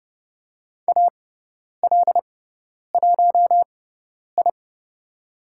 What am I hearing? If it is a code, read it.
Morse code, heard as a keyed tone, about 30 wpm, spaced as for 10 wpm, 700 Hz. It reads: AL1I
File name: Lesson 16g - Letter c - Call Signs 30wpm